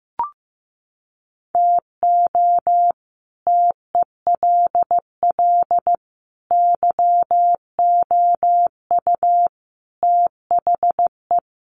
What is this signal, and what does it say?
Morse code, heard as a keyed tone, about 15 wpm, 700 Hz. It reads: TO TELL YOU THE
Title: Sets of 5 Words 15wpm